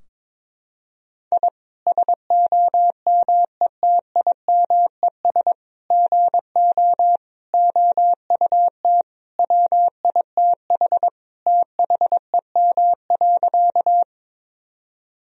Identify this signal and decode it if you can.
Morse code, heard as a keyed tone, about 22 wpm, 700 Hz. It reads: I SOMETIMES GO OUT WITH THEM.
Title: Sentences from Top 1000 Words - Review Part 3 - 22wpm